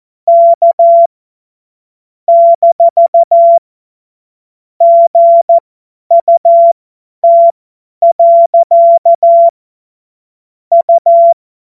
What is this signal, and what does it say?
Morse code, heard as a keyed tone, about 14 wpm, spaced as for 10 wpm, 680 Hz. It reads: K - GUT. U
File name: Morsekurs / Witze / Witz 30 Januar 2024 50 F70